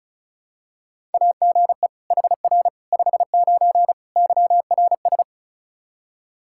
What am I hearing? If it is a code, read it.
Morse code, heard as a keyed tone, about 36 wpm, 700 Hz. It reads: AGE HR 59 YRS
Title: Practice QSO - Exchange 3 - More Info - 35wpm